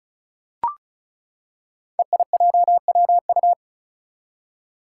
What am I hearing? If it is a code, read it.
Morse code, heard as a keyed tone, about 36 wpm, 700 Hz. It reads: EIJWU
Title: Five Challenging Letter-Number Combinations 35wpm